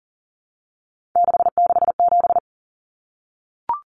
Morse code, heard as a keyed tone, about 40 wpm, 700 Hz. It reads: 667
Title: Common Cut Numbers - 3-Digit 40wpm